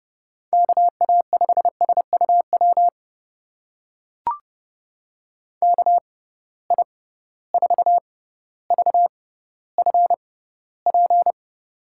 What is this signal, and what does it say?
Morse code, heard as a keyed tone, about 30 wpm, spaced as for 15 wpm, 700 Hz. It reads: KA5SUW KI4VFP
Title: USA Call Signs - 6 Characters - 6x Character Spacing - 30wpm